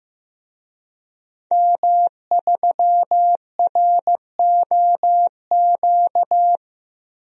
Morse code, heard as a keyed tone, about 16 wpm, 700 Hz. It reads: M3ROQ
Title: DXCC-100 Call Signs - 5 Characters - 15wpm